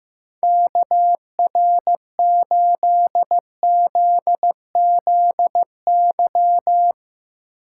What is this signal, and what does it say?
Morse code, heard as a keyed tone, about 15 wpm, 700 Hz. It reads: KR8ZZY
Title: USA Call Signs - 6 Characters - 8x Character Spacing - 15wpm